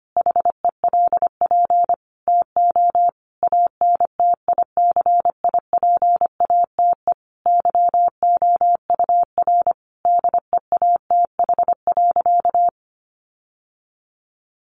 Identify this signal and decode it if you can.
Morse code, heard as a keyed tone, about 25 wpm, 700 Hz. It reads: HELP TO ANTICIPATE YOUR DEATH.
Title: Sentences from Top 1500 Words 25wpm